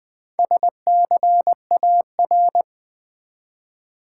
Morse code, heard as a keyed tone, about 20 wpm, 700 Hz. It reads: SCAR
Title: Build A Word - 1 Letter At A Time - 3 to 5-Letter Words - 20wpm